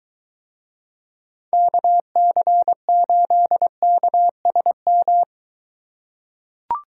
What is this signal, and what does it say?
Morse code, heard as a keyed tone, about 22 wpm, 700 Hz. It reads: KC8KSM